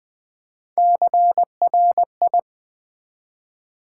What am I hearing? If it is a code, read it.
Morse code, heard as a keyed tone, about 20 wpm, 700 Hz. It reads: CRI
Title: Build A Word - 1 Letter At A Time - 3 to 5-Letter Words - 20wpm